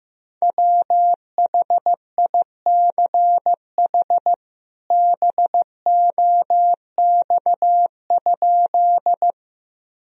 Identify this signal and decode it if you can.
Morse code, heard as a keyed tone, about 15 wpm, 700 Hz. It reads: WHICH BOX?
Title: Sentences from Top 1000 Words - Review Part 2 - 15wpm